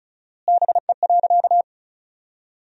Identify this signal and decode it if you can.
Morse code, heard as a keyed tone, about 36 wpm, 700 Hz. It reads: BE.
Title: 4x Word-Spacing - Sentences from Top 200 Words - 35wpm